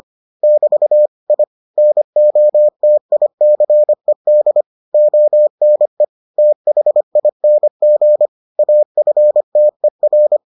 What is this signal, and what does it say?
Morse code, heard as a keyed tone, about 25 wpm, 600 Hz. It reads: <BT> I NOTICED ONE THING AFTER